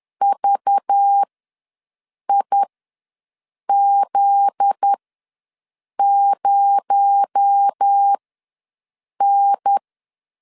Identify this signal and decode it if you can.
Morse code, heard as a keyed tone, about 10 wpm, spaced as for 6 wpm, 800 Hz. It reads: VIZ0N
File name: LEZIONI / 7 GF9Z / 7 GF9Z ws 10wpm 3919